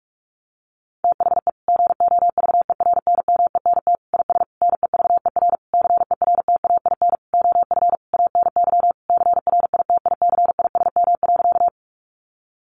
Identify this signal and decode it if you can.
Morse code, heard as a keyed tone, about 44 wpm, 700 Hz. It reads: THE GOVERNMENT IS NEVER CERTAIN OF ANY CRITICISM.
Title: Sentences from Top 1300 Words 45wpm